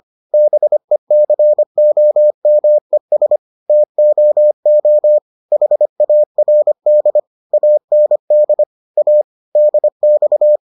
Morse code, heard as a keyed tone, about 25 wpm, 600 Hz. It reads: BECOMES TOO HARD AND A DX